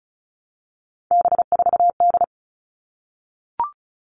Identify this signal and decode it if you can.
Morse code, heard as a keyed tone, about 34 wpm, 700 Hz. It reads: B4D
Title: All Cut Numbers - 3-Digit 35wpm